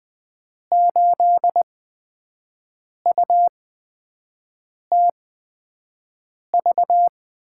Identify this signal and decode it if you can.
Morse code, heard as a keyed tone, about 20 wpm, spaced as for 5 wpm, 700 Hz. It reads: 8UTV
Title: USA Call Signs - 6 Characters - 8x Character Spacing - 20wpm